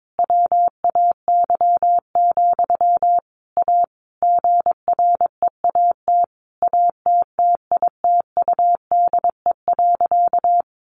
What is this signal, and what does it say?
Morse code, heard as a keyed tone, about 22 wpm, 700 Hz. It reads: WAY, A GREAT ATTITUDE.
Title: Sentences from Top 1300 Words 22wpm